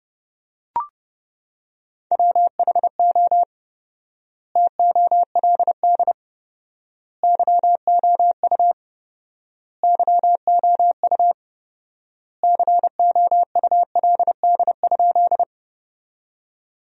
Morse code, heard as a keyed tone, about 30 wpm, 700 Hz. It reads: WHO TOLD YOU YOU COULD?
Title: 4x Word-Spacing - Sentences from Top 400 Words - 30wpm